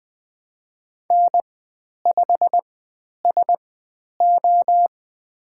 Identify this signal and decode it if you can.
Morse code, heard as a keyed tone, about 20 wpm, spaced as for 10 wpm, 700 Hz. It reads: N5SO
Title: 102 Lesson 17g - Letter u - Call Signs 20wpm